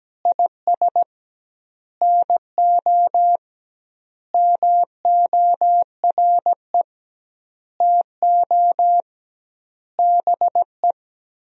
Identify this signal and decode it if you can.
Morse code, heard as a keyed tone, about 17 wpm, 700 Hz. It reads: IS NO MORE TO BE
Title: 2x Word-Spacing - Sentences from Top 100 Words - 17wpm